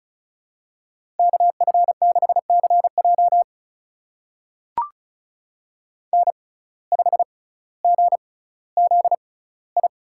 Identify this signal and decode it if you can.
Morse code, heard as a keyed tone, about 34 wpm, spaced as for 18 wpm, 700 Hz. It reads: KF6CJ N5GZI
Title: USA Call Signs - 5 Characters - 6x Character Spacing - 35wpm